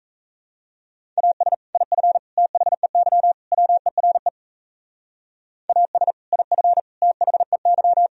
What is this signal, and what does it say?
Morse code, heard as a keyed tone, about 35 wpm, 700 Hz. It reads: AS IF THEY WERE AS IF THEY